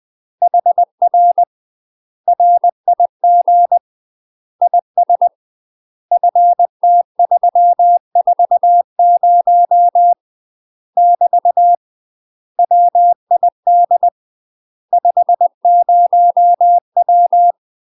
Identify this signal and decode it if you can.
Morse code, heard as a keyed tone, about 20 wpm, 700 Hz. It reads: HR RIG IS FT340 <BT> WID 50W
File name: UK MASTER QSOs TRAINING / 20WPM / UK MASTER QSOs 0017